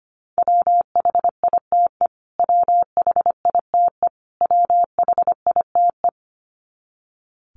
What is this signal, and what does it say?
Morse code, heard as a keyed tone, about 25 wpm, 700 Hz. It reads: WHITE WHITE WHITE
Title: Top 500 Words Repeated 25wpm